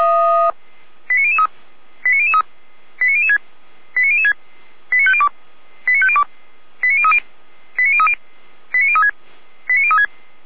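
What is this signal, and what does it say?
Five-tone selective call (ZVEI), heard as quick runs of five tones, tones of about 80 ms. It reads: 89003 89003 89006 89006 89502 89502 89030 89030 89036 89036
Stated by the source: ZVEI five-tone selective call 89099, received by SDR.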